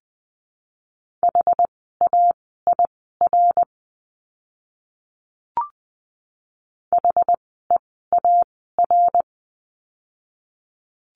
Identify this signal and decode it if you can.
Morse code, heard as a keyed tone, about 20 wpm, spaced as for 14 wpm, 700 Hz. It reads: HAIR HEAR
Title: 101 Lesson 17f - Letter u - Words 20wpm